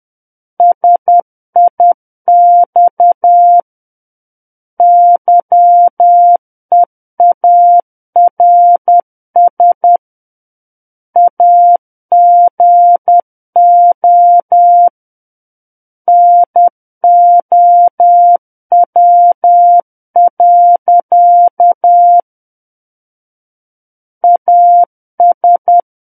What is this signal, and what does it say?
Morse code, heard as a keyed tone, about 10 wpm, 700 Hz. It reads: SIX YEARS AGO NOW. AS
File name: Book One / 01-Chapter 1 - 10 WPM